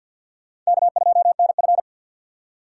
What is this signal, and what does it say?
Morse code, heard as a keyed tone, about 64 wpm, 700 Hz. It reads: K2NF